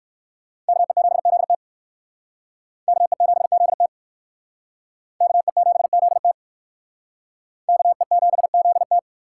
Morse code, heard as a keyed tone, about 59 wpm, 700 Hz. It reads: KE7ZT KE7ZT KE7ZT KE7ZT